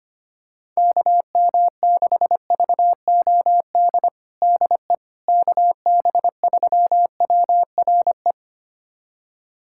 Morse code, heard as a keyed tone, about 25 wpm, 700 Hz. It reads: KM6VOD DE KB3WRE
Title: USA Callsign DE Callsign 25wpm